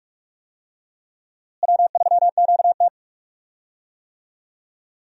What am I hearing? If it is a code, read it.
Morse code, heard as a keyed tone, about 46 wpm, 700 Hz. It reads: W3QT
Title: USA Call Signs - 4 Characters - Repeated 2x - 45wpm